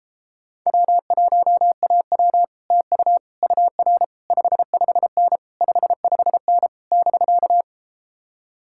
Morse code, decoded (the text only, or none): W1AW TU UR 55N 55N <BK>